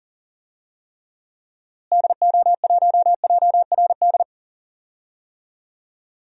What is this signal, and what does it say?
Morse code, heard as a keyed tone, about 40 wpm, 700 Hz. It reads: DO1JRD
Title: DXCC-100 Call Signs - 6 Characters - Repeated 3x - 40wpm